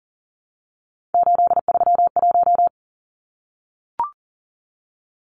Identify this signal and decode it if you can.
Morse code, heard as a keyed tone, about 40 wpm, 700 Hz. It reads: 831
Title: Numbers - 3-Digit 40wpm